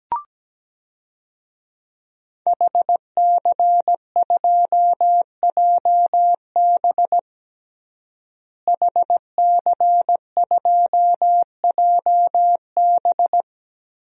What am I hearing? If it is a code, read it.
Morse code, heard as a keyed tone, about 17 wpm, 700 Hz. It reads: HC2JB HC2JB
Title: DXCC-100 Call Signs - 5 Characters - Repeated 2x - 17wpm